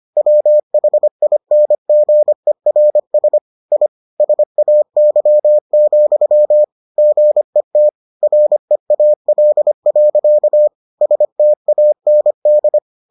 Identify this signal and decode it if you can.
Morse code, heard as a keyed tone, about 25 wpm, 600 Hz. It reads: WHINGERS I SAY, GET REAL. STAND